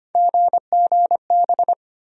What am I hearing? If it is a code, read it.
Morse code, heard as a keyed tone, about 25 wpm, 700 Hz. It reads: GGB
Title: USA Call Sign Postfixes - 3 Letters 25wpm